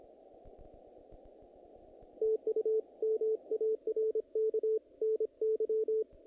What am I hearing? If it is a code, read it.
Morse code, heard as a keyed tone, about 25 wpm, 440 Hz. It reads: TU MARK NY
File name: contests / naqp-cw-k5tr-op-k5pi- -2000